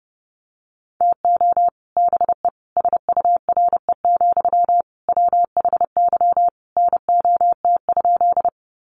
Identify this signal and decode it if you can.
Morse code, heard as a keyed tone, about 30 wpm, 700 Hz. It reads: TO BE SURE, WHY NOT?
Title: Sentences from Top 600 Words 30wpm